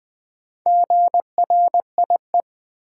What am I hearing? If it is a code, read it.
Morse code, heard as a keyed tone, about 20 wpm, 700 Hz. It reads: GRIE
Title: Build A Word - 1 Letter At A Time - 3 to 5-Letter Words - 20wpm